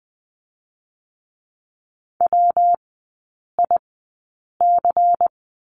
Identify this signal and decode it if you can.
Morse code, heard as a keyed tone, about 20 wpm, spaced as for 8 wpm, 700 Hz. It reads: WIC
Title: 114 Lesson 19e - Letter w - Three Characters 20wpm